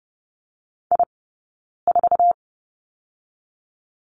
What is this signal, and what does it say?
Morse code, heard as a keyed tone, about 30 wpm, 700 Hz. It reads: I 4
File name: Lesson 20d - Number 3 - Two Characters 30wpm